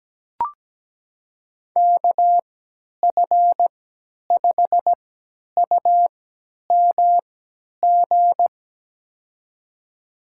Morse code, heard as a keyed tone, about 17 wpm, spaced as for 10 wpm, 700 Hz. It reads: KF5UMG
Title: USA Call Signs - 6 Characters - 3x Character Spacing - 17wpm